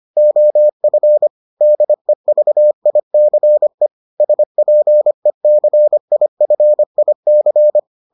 Morse code, heard as a keyed tone, about 25 wpm, 600 Hz. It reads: OF DEVICE SPECIFIC